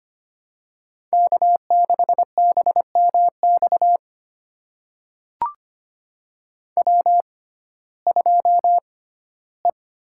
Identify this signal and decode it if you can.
Morse code, heard as a keyed tone, about 25 wpm, spaced as for 15 wpm, 700 Hz. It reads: K6BMX W2E
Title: USA Call Signs - 5 Characters - 6x Character Spacing - 25wpm